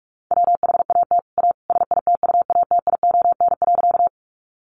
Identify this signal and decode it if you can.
Morse code, heard as a keyed tone, about 44 wpm, 700 Hz. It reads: WHAT A SITUATION.
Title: Sentences from Top 1200 Words 45wpm